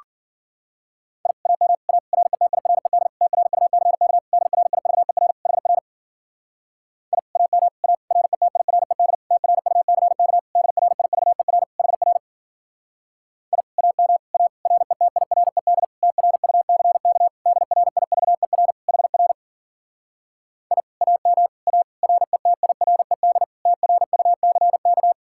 Morse code, heard as a keyed tone, about 50 wpm, 700 Hz. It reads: I AM A RETIRED TRUCK DRIVER HR I AM A RETIRED TRUCK DRIVER HR I AM A RETIRED TRUCK DRIVER HR I AM A RETIRED TRUCK
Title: Speed-Racing Common QSO Phrases Encore - 40wpm